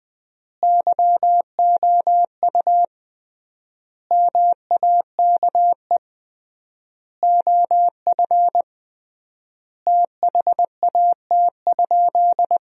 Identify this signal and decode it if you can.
Morse code, heard as a keyed tone, about 20 wpm, 700 Hz. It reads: YOU MAKE OF THAT?
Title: 3x Word-Spacing - Sentences from Top 100 Words - 20wpm